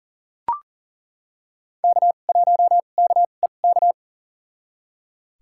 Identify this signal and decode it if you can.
Morse code, heard as a keyed tone, about 40 wpm, spaced as for 29 wpm, 700 Hz. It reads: K1KEK
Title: USA Call Signs - 5 Characters - 2x Character Spacing - 40wpm